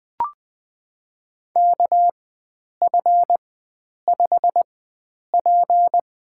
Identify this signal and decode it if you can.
Morse code, heard as a keyed tone, about 20 wpm, spaced as for 9 wpm, 700 Hz. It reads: KF5P
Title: USA Call Signs - 4x Character Spacing - 20wpm